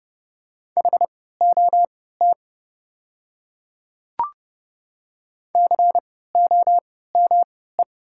Morse code, heard as a keyed tone, about 30 wpm, spaced as for 17 wpm, 700 Hz. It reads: HOT COME
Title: 3x Character Spacing - Top 100 Words - 30wpm